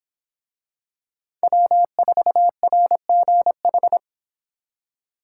Morse code, heard as a keyed tone, about 23 wpm, 700 Hz. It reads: W4RGH